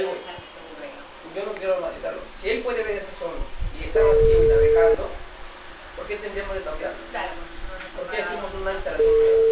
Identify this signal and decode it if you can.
Ringback tone (China, 450 Hz), with speech.